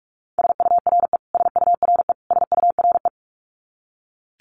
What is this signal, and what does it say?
Morse code, heard as a keyed tone, about 44 wpm, 700 Hz. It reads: SURE SURE SURE
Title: Top 500 Words Repeated 45wpm